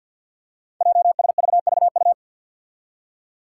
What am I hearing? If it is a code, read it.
Morse code, heard as a keyed tone, about 53 wpm, 700 Hz. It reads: JSVVU
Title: Five Challenging Letter-Number Combinations 50wpm